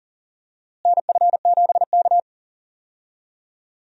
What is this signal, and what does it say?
Morse code, heard as a keyed tone, about 40 wpm, 700 Hz. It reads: NF7K
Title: USA Call Signs - 4 Characters - Repeated 2x - 40wpm